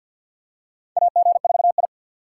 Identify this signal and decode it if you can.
Morse code, heard as a keyed tone, about 48 wpm, 700 Hz. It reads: AG4I
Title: USA Call Signs - 4 Characters - 3x Character Spacing - 50wpm